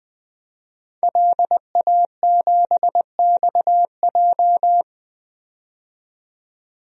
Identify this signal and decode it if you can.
Morse code, heard as a keyed tone, about 20 wpm, 700 Hz. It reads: LA7XJ
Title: DXCC-100 Call Signs - 5 Characters - 20wpm